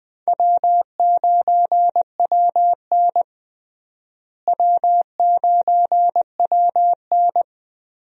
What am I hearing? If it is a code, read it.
Morse code, heard as a keyed tone, about 20 wpm, 700 Hz. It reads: W9WN W9WN
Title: USA Call Signs - 4 Characters - Repeated 3x - 20wpm